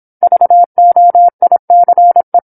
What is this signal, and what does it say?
Morse code, heard as a keyed tone, about 27 wpm, 700 Hz. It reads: VOICE